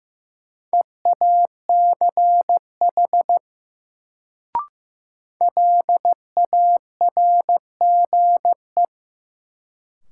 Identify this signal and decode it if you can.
Morse code, heard as a keyed tone, about 15 wpm, 700 Hz. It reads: EACH LARGE
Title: Sentences from Top 200 Words - New Word Drill - 15wpm